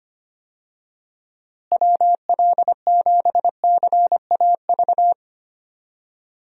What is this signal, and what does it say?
Morse code, heard as a keyed tone, about 25 wpm, 700 Hz. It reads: WL7CAV
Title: USA Call Signs - 6 Characters - Repeated 3x - 25wpm